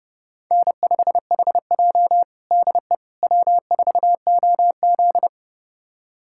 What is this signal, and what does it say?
Morse code, heard as a keyed tone, about 32 wpm, 700 Hz. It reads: N5HJ DE W4OZ
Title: USA Callsign DE Callsign - 4 Characters - 30wpm